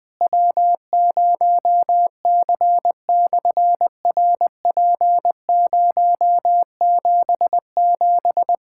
Morse code, heard as a keyed tone, about 20 wpm, 700 Hz. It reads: W0C/RP077
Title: SOTA QSO Practice 20wpm